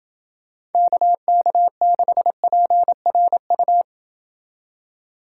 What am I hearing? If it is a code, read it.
Morse code, heard as a keyed tone, about 30 wpm, 700 Hz. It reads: KK6PRU